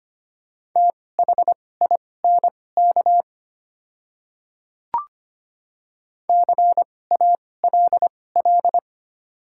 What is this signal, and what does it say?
Morse code, heard as a keyed tone, about 25 wpm, spaced as for 18 wpm, 690 Hz. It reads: THINK CALL
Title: 2x Character Spacing - Top 200 Words - 25wpm